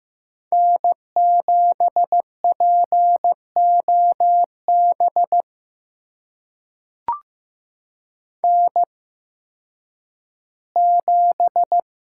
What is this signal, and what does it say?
Morse code, heard as a keyed tone, about 15 wpm, spaced as for 10 wpm, 700 Hz. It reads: N7POB N7
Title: USA Call Signs - 5 Characters - 8x Character Spacing - 15wpm